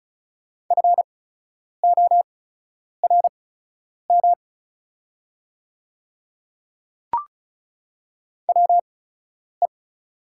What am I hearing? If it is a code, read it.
Morse code, heard as a keyed tone, about 36 wpm, spaced as for 10 wpm, 700 Hz. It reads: FORM WE
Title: 8x Character Spacing - Top 200 Words - 35wpm